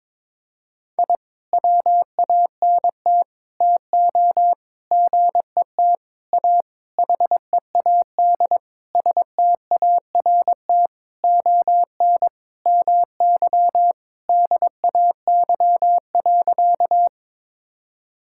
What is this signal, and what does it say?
Morse code, heard as a keyed tone, about 22 wpm, 700 Hz. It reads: I WANT TO GET A HEAD START ON MY DAY.
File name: Sentences with 9-Word Phrases - 22wpm